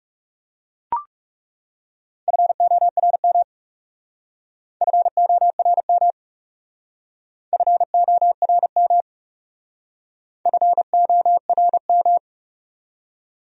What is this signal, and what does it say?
Morse code, heard as a keyed tone, about 37 wpm, 700 Hz. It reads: FORM FORM FORM FORM
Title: Speed-Racing Top 500 Words 30wpm